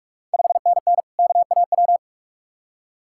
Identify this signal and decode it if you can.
Morse code, heard as a keyed tone, about 46 wpm, 690 Hz. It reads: 5NN KAW
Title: Ontario QSO Party Exchange 45wpm